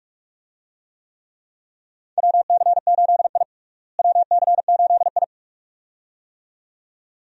Morse code, heard as a keyed tone, about 43 wpm, 700 Hz. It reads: WC8I WC8I
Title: USA Call Signs - 4 Characters - Repeated 2x - 45wpm